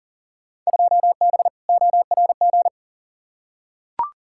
Morse code, heard as a keyed tone, about 40 wpm, 700 Hz. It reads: 2B ORG
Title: Field Day Exchanges Encore - 40wpm